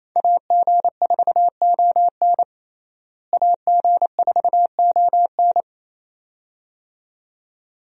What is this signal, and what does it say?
Morse code, heard as a keyed tone, about 28 wpm, 700 Hz. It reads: AG4ON AG4ON
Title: USA Call Signs - 5 Characters - Repeated 2x - 28wpm